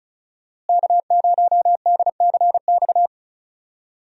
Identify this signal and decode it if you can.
Morse code, heard as a keyed tone, about 36 wpm, 700 Hz. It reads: K0DCX
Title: USA Call Signs - 5 Characters - Repeated 2x - 35wpm